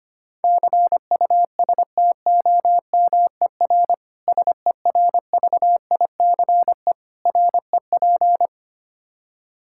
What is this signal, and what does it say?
Morse code, heard as a keyed tone, about 25 wpm, 700 Hz. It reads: CUSTOMER SERVICE REP
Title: K1USN SST QSO Practice 25wpm